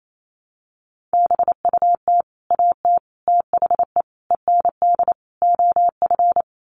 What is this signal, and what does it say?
Morse code, heard as a keyed tone, about 28 wpm, 700 Hz. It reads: BUT AT THE END OF
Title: Sets of 5 Words Encore2 28wpm